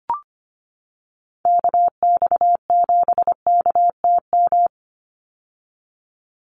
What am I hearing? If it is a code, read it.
Morse code, heard as a keyed tone, about 25 wpm, 700 Hz. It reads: KX7KTM
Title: USA Call Signs 25wpm